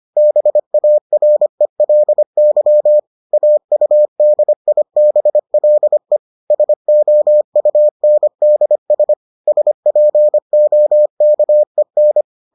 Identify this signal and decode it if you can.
Morse code, heard as a keyed tone, about 25 wpm, 600 Hz. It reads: BARELY AUDIBLE SOUNDS SPOKEN